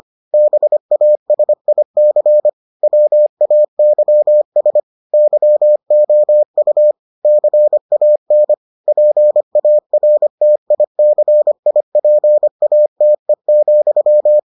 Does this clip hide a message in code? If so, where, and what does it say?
Yes: Morse code from 0.3 to 14.4 s, heard as a keyed tone, about 25 wpm, 600 Hz. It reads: BASIC WAYS YOU CAN PARTICIPATE,